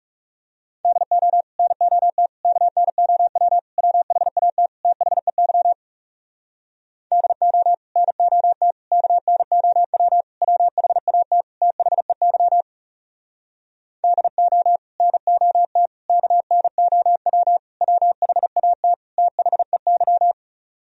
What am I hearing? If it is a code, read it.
Morse code, heard as a keyed tone, about 40 wpm, 700 Hz. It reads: DO NOT KNOW WHAT THEY DO NOT KNOW WHAT THEY DO NOT KNOW WHAT THEY